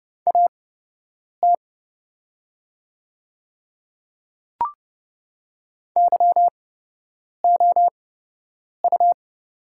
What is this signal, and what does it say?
Morse code, heard as a keyed tone, about 30 wpm, spaced as for 8 wpm, 700 Hz. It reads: AT YOU
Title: 8x Character Spacing - Top 100 Words - 30wpm